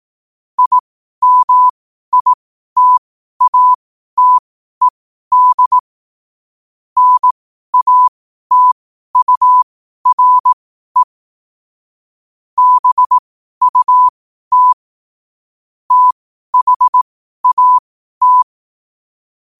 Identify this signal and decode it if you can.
Morse code, heard as a keyed tone, about 18 wpm, spaced as for 12 wpm, 1000 Hz. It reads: IMITATED NATURE BUT THAT
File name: qotdfiles / 12wpm